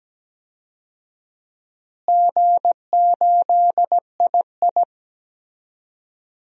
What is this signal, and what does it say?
Morse code, heard as a keyed tone, about 17 wpm, 700 Hz. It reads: G8II
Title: DXCC-100 Call Signs - 4 Characters - Repeated 3x - 17wpm